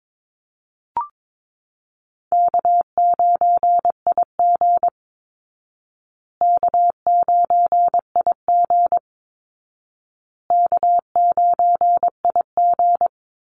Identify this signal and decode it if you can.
Morse code, heard as a keyed tone, about 22 wpm, 700 Hz. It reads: K9IG K9IG K9IG
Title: Easy Call Signs Repeated 3x - 22wpm